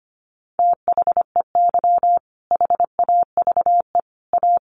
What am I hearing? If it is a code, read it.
Morse code, heard as a keyed tone, about 25 wpm, 700 Hz. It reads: THEY HAVE A
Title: Sentences from Top 900 Words 25wpm